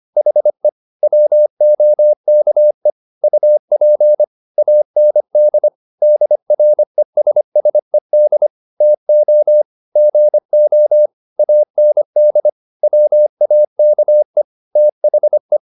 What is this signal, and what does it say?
Morse code, heard as a keyed tone, about 25 wpm, 600 Hz. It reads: HE WOKE UP AND DRESSED TO GO AND WAKE THE
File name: the old man and the sea part 1 25wpm